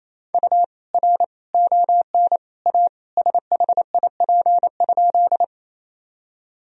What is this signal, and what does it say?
Morse code, heard as a keyed tone, about 27 wpm, 700 Hz. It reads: U R ON A SHIP?
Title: Sentences from Top 400 Words and Abbreviations 28wpm